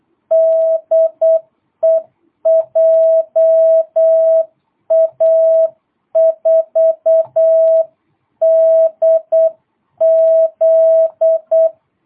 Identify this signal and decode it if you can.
Morse code, heard as a keyed tone, about 8 wpm, 660 Hz. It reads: DE JA4DZ
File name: jh8cbh / 2022cw